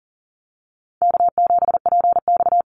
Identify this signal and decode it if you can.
Morse code, heard as a keyed tone, about 42 wpm, 700 Hz. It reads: K7PX
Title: NA SOTA Activator Callsigns 40wpm